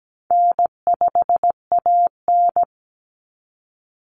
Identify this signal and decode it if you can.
Morse code, heard as a keyed tone, about 17 wpm, 700 Hz. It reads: N5AN
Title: Easy Call Signs Repeated 3x - 17wpm